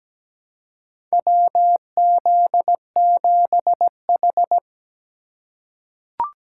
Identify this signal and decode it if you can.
Morse code, heard as a keyed tone, about 17 wpm, 700 Hz. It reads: WZ7H
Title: USA Call Signs - 4 Characters - 4x Character Spacing - 17wpm